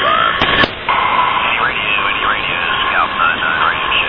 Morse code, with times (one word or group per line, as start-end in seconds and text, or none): none